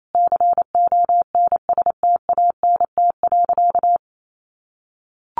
Morse code, heard as a keyed tone, about 28 wpm, 700 Hz. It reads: CONSTANT.
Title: Sentences from Top 1500 Words 28wpm